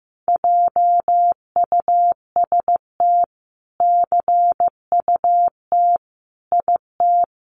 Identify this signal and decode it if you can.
Morse code, heard as a keyed tone, about 15 wpm, 700 Hz. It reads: JUST CUT IT
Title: Sentences from Top 300 Words 15wpm